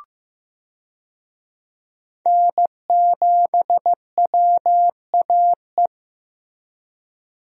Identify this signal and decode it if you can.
Morse code, heard as a keyed tone, about 15 wpm, 700 Hz. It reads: N7WAE
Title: USA Call Signs - 5 Characters - Repeated 2x - 15wpm